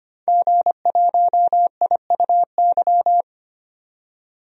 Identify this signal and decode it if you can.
Morse code, heard as a keyed tone, about 25 wpm, 700 Hz. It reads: G1IUY
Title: DXCC-100 Call Signs - 5 Characters - 25wpm